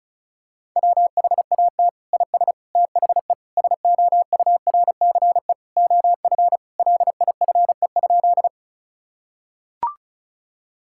Morse code, heard as a keyed tone, about 36 wpm, 700 Hz. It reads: WHAT IS THE SOURCE OF LIFE?
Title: Sentences from Top 1000 Words - Review Part 5 - 35wpm